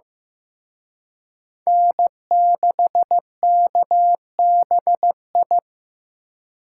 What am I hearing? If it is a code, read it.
Morse code, heard as a keyed tone, about 18 wpm, 700 Hz. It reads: N6KBI
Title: Speed Racing - USA 1x3 Call Signs 15wpm